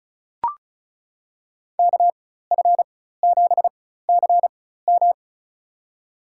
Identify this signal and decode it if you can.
Morse code, heard as a keyed tone, about 36 wpm, spaced as for 16 wpm, 700 Hz. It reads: KF7CM
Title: USA Call Signs - 5 Characters - 4x Character Spacing - 35wpm